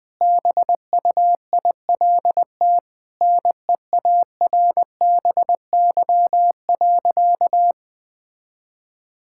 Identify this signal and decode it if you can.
Morse code, heard as a keyed tone, about 20 wpm, 700 Hz. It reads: BUILT NEARBY.
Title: Sentences with 2-Word Phrases - 20wpm